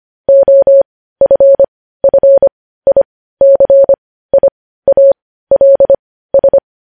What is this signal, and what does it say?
Morse code, heard as a keyed tone, about 25 wpm, spaced as for 15 wpm, 550 Hz. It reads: OFFICIALS
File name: podcast / News headlines-1x-25x15